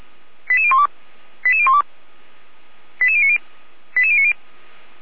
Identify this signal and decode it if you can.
Five-tone selective call (ZVEI), heard as quick runs of five tones, tones of about 70 ms. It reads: 88012 88012 88090 88090